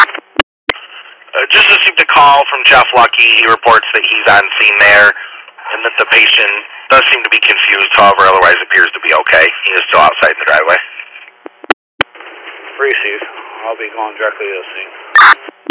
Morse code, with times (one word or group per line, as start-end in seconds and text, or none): none